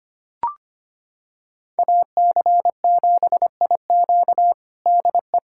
Morse code, heard as a keyed tone, about 25 wpm, 700 Hz. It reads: AC7IQ DE
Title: USA Callsign DE Callsign - 5 Characters - 25wpm